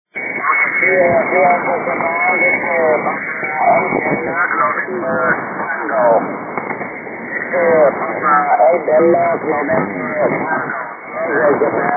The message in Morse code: none